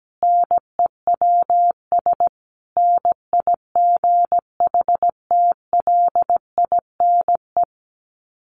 Morse code, heard as a keyed tone, about 17 wpm, 700 Hz. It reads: NEWS NIGHTLINE
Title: Sets of 5 Words 17wpm